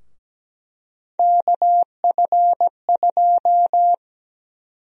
Morse code, heard as a keyed tone, about 17 wpm, 700 Hz. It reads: KF2
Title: USA Call Sign Prefixes - Repeated 2x - 17wpm